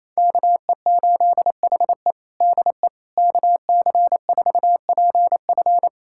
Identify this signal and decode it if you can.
Morse code, heard as a keyed tone, about 28 wpm, 700 Hz. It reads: KE8HE DE KC4PF
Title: USA Callsign DE Callsign - 5 Characters - 28wpm